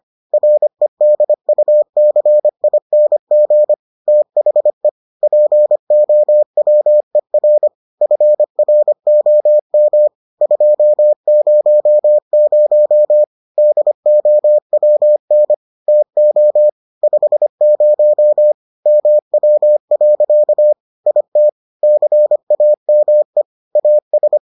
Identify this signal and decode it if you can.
Morse code, heard as a keyed tone, about 25 wpm, 600 Hz. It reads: REDUCING THE POWER FROM 200 DOWN TO 50 MW. IT CAME AS